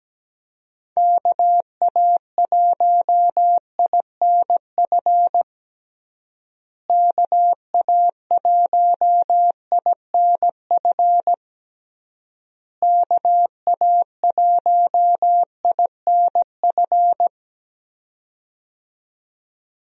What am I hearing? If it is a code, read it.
Morse code, heard as a keyed tone, about 17 wpm, 700 Hz. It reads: KA1INF KA1INF KA1INF
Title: USA Call Signs - 6 Characters - Repeated 3x - 17wpm